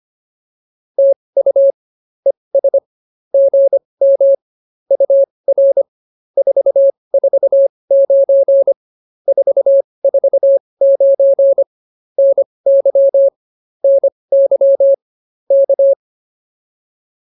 Morse code, heard as a keyed tone, about 24 wpm, spaced as for 20 wpm, 550 Hz. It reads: TU ES GM UR 449 449 NY NY K
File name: POTA206 20